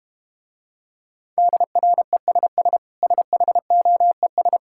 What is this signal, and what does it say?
Morse code, heard as a keyed tone, about 32 wpm, 700 Hz. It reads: DRESS SHOES